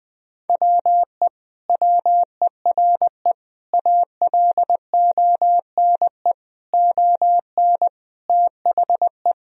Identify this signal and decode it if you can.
Morse code, heard as a keyed tone, about 20 wpm, 700 Hz. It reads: WE WERE ALONE ON THE
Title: Sentences from Top 1000 Words - Review Part 3 - 20wpm